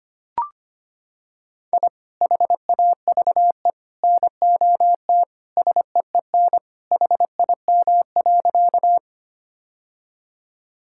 Morse code, heard as a keyed tone, about 25 wpm, 700 Hz. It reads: I HAVE NOT SEEN HIM.
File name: practice / 25wpm / Sentences with 3-Word Phrases - 25wpm